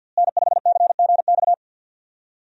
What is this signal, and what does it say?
Morse code, heard as a keyed tone, about 52 wpm, 700 Hz. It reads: N5CGX
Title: USA Call Signs - 5 Characters - Repeated 3x - 50wpm